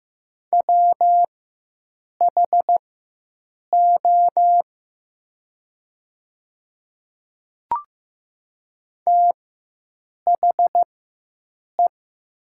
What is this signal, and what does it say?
Morse code, heard as a keyed tone, about 15 wpm, spaced as for 7 wpm, 700 Hz. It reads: WHO THE
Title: 4x Character Spacing - Top 200 Words - 15wpm